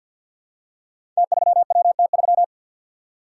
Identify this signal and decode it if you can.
Morse code, heard as a keyed tone, about 52 wpm, 700 Hz. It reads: T3WT3
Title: Code Groups - 5 Letter:Numbers - 50wpm